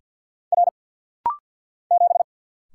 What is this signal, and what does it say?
Morse code, heard as a keyed tone, about 49 wpm, 700 Hz. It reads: R 7
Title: Single Letter-Number - Rapid-Fire 50wpm